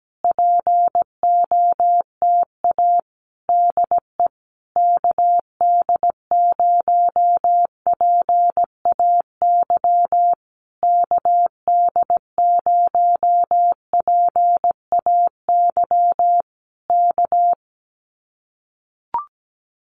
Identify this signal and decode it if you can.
Morse code, heard as a keyed tone, about 17 wpm, 700 Hz. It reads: POTA DE KD0PAY KD0PAY K
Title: Common QSO Phrases Encore 17wpm